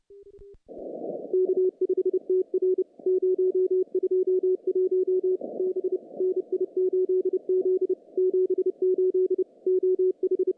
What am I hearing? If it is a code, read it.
Morse code, heard as a keyed tone, about 28 wpm, 370 Hz. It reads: K5TR 021 B NI8Z 78 OH
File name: contests / CW-SS-k5tr- -0000